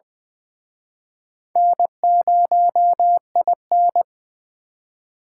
Morse code, heard as a keyed tone, about 25 wpm, 700 Hz. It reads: N0IN
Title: Speed Racing - USA 1x2 Call Signs 20wpm